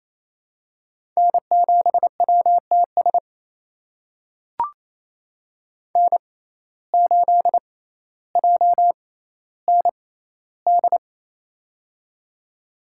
Morse code, heard as a keyed tone, about 28 wpm, spaced as for 14 wpm, 700 Hz. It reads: N7WTS N8JND
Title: USA Call Signs - 5 Characters - 6x Character Spacing - 28wpm